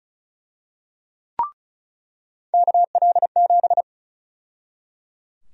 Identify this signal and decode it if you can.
Morse code, heard as a keyed tone, about 34 wpm, 700 Hz. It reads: KL7
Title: USA Call Sign Prefixes - 35wpm